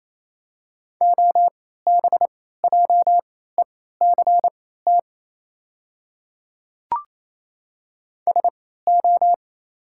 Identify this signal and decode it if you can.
Morse code, heard as a keyed tone, about 28 wpm, spaced as for 16 wpm, 700 Hz. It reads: OBJECT SO
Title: 3x Character Spacing - Top 500 Words - 28wpm